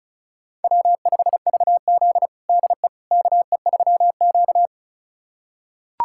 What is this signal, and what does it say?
Morse code, heard as a keyed tone, about 36 wpm, 700 Hz. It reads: W5VZ DE KE3Q
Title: USA Callsign DE Callsign - 4 Characters - 35wpm